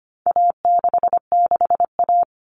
Morse code, heard as a keyed tone, about 25 wpm, 700 Hz. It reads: A66A
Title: All Cut Numbers - 4-Digit 25wpm